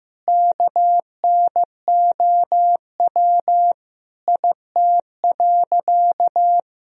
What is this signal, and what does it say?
Morse code, heard as a keyed tone, about 16 wpm, 700 Hz. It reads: KNOW IT.
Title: Sentences from Top 1000 Words - Review Part 2 - 15wpm